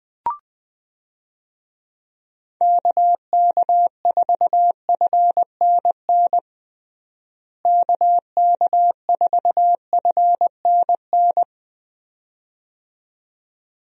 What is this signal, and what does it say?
Morse code, heard as a keyed tone, about 20 wpm, 700 Hz. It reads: KK4FNN KK4FNN
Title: USA Call Signs - 6 Characters - Repeated 2x - 20wpm